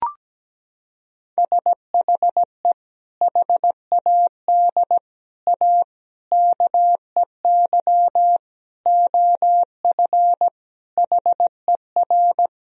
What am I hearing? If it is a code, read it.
Morse code, heard as a keyed tone, about 17 wpm, 700 Hz. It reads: SHE HAD A KEY OF HER
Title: Sentences from Top 1000 Words - Review Part 4 - 17wpm